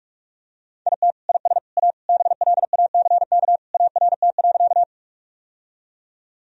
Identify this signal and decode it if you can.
Morse code, heard as a keyed tone, about 46 wpm, 700 Hz. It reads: IT IS A BLACK ART.
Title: Sentences from Top 1000 Words - Review Part 3 - 45wpm